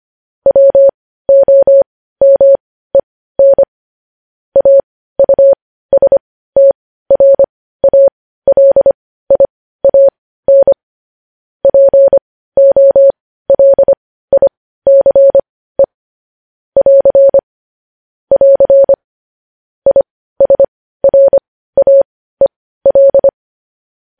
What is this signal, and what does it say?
Morse code, heard as a keyed tone, about 25 wpm, spaced as for 15 wpm, 550 Hz. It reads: WOMEN AUSTRALIAN POLICE <AR> <AR> ISRAEL